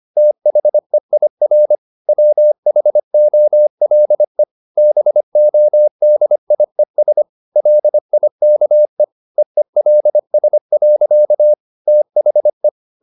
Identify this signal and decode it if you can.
Morse code, heard as a keyed tone, about 25 wpm, 600 Hz. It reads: THEIR WHOLE BODIES LIKE EELS. THE